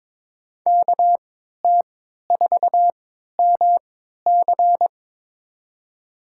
Morse code, heard as a keyed tone, about 22 wpm, spaced as for 12 wpm, 700 Hz. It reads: KT4MC
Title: USA Call Signs - 5 Characters - 3x Character Spacing - 22wpm